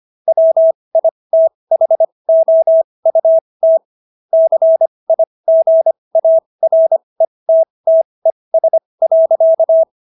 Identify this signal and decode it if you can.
Morse code, heard as a keyed tone, about 25 wpm, spaced as for 20 wpm, 650 Hz. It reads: WITHOUT CIGARETTES.